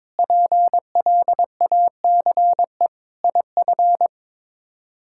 Morse code, heard as a keyed tone, about 22 wpm, 700 Hz. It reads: PLACE IF
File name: practice / 22wpm / Sets of 8 Words - 22wpm